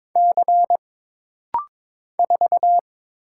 Morse code, heard as a keyed tone, about 22 wpm, 700 Hz. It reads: C 4
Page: This rapid-fire timing is geared to push your ICR (Instant Character Recognition) skills.